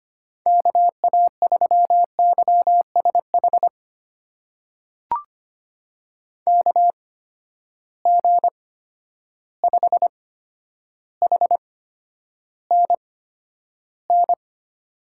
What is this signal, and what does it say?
Morse code, heard as a keyed tone, about 25 wpm, spaced as for 11 wpm, 700 Hz. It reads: KA3YSH KG5HNN
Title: USA Call Signs - 6 Characters - 8x Character Spacing - 25wpm